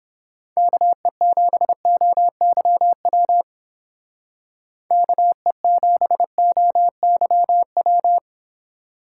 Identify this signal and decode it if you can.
Morse code, heard as a keyed tone, about 31 wpm, 700 Hz. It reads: KE7OYW KE7OYW